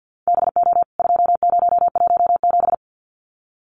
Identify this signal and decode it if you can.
Morse code, heard as a keyed tone, about 50 wpm, 700 Hz. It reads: BO 2017
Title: CWT Exchange 50wpm